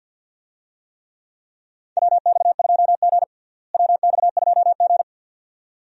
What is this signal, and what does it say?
Morse code, heard as a keyed tone, about 50 wpm, 700 Hz. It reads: WX2G WX2G
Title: USA Call Signs - 4 Characters - Repeated 2x - 50wpm